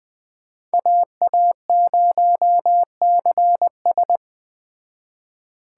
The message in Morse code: AA0CS

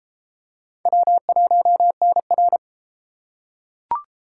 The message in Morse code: W1NR

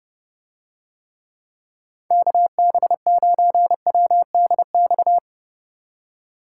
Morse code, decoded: KB9WDX

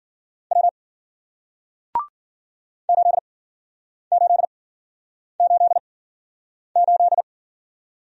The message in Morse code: W 8888

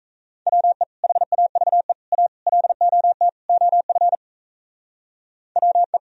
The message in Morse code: WE HAVE A LOT OF WE

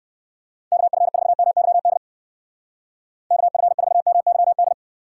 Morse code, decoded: KF4MYD KF4MYD

